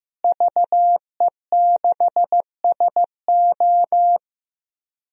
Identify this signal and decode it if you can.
Morse code, heard as a keyed tone, about 15 wpm, 700 Hz. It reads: VE6SO
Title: DXCC-100 Call Signs - 5 Characters - Repeated 3x - 15wpm